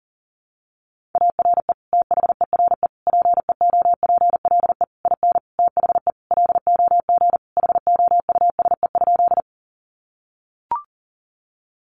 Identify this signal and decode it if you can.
Morse code, heard as a keyed tone, about 40 wpm, 700 Hz. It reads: ARE THERE PEOPLE IN THE LOG HOUSE?
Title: Sentences from Top 1400 Words 40wpm